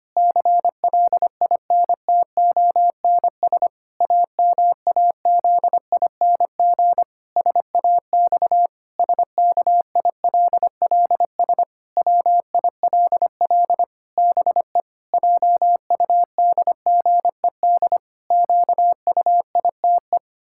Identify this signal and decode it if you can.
Morse code, heard as a keyed tone, about 25 wpm, 700 Hz. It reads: CLINTONS AMAZING SAX SKILLS WILL BE JUDGED QUITE
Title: Pangram Challenge v2 - 25wpm